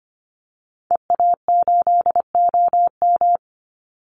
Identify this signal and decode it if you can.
Morse code, heard as a keyed tone, about 25 wpm, 700 Hz. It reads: EA8OM
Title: Call Signs Repeated 3x - 25wpm